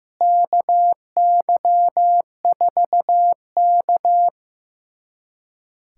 Morse code, heard as a keyed tone, about 15 wpm, 700 Hz. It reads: KY4K
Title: USA Callsign DE Callsign - 4 Characters - 15wpm